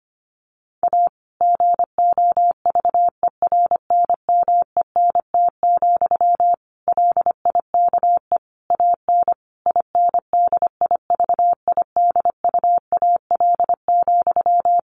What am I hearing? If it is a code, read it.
Morse code, heard as a keyed tone, about 25 wpm, 700 Hz. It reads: A GOVERNMENT, LIKE AN INDIVIDUAL,
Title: Sentences from Top 1400 Words 25wpm